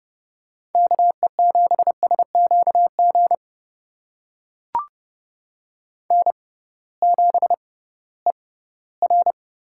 Morse code, heard as a keyed tone, about 30 wpm, spaced as for 18 wpm, 700 Hz. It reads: KE7SQG N7ER
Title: USA Call Signs - 6x Character Spacing - 30wpm